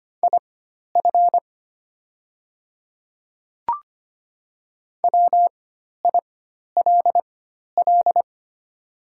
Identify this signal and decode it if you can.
Morse code, heard as a keyed tone, about 25 wpm, spaced as for 12 wpm, 700 Hz. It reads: IF WILL
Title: 4x Character Spacing - Top 100 Words - 25wpm